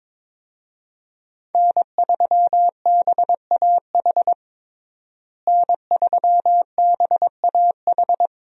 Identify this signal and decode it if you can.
Morse code, heard as a keyed tone, about 22 wpm, 700 Hz. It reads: N3BAH N3BAH
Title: USA Call Signs - 5 Characters - Repeated 2x - 22wpm